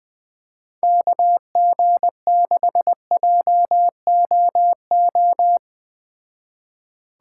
Morse code, heard as a keyed tone, about 20 wpm, 700 Hz. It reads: KG6JOO